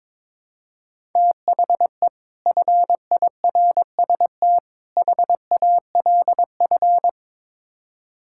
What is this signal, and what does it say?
Morse code, heard as a keyed tone, about 22 wpm, 700 Hz. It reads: THE FIRST HALF
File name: practice / 22wpm / Build A Phrase - 5-Word Phrases - 22wpm